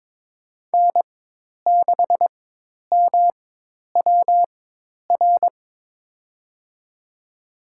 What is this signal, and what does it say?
Morse code, heard as a keyed tone, about 23 wpm, spaced as for 10 wpm, 700 Hz. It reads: N6MWR
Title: USA Call Signs - 5 Characters - 4x Character Spacing - 22wpm